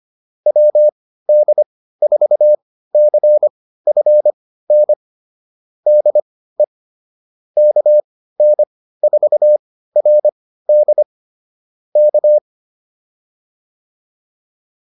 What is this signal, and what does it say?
Morse code, heard as a keyed tone, about 25 wpm, spaced as for 15 wpm, 600 Hz. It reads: WD4CFN DE KN4RD K